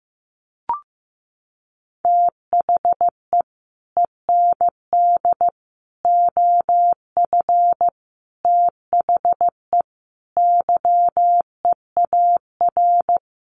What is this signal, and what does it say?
Morse code, heard as a keyed tone, about 15 wpm, 700 Hz. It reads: THE END OF THE YEAR
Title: Sets of 5 Words 15wpm